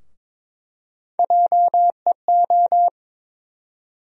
Morse code, heard as a keyed tone, about 22 wpm, 700 Hz. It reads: JEO
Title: USA Call Sign Postfixes - 3 Letters 22wpm